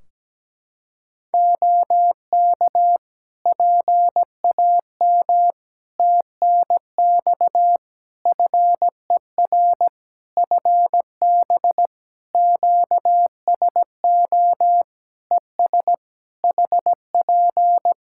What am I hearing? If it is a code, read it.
Morse code, heard as a keyed tone, about 17 wpm, 700 Hz. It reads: OK PAM TNX FER FB QSO ES HP
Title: K1USN SST QSO Practice 17wpm